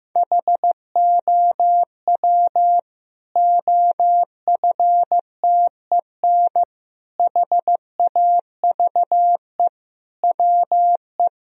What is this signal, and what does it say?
Morse code, heard as a keyed tone, about 15 wpm, 700 Hz. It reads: HOW OFTEN HAVE WE
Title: Sentences from Top 1000 Words - Review Part 1 - 15wpm